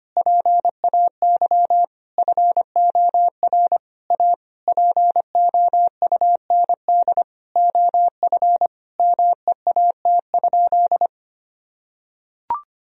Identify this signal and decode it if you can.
Morse code, heard as a keyed tone, about 25 wpm, 700 Hz. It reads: PAY FOR A POUND OF MEAT?
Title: Sentences from Top 1000 Words - Review Part 3 - 25wpm